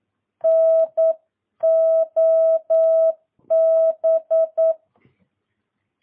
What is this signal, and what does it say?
Morse code, heard as a keyed tone, about 9 wpm, 650 Hz. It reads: NOB
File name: jh8cbh / 2022cw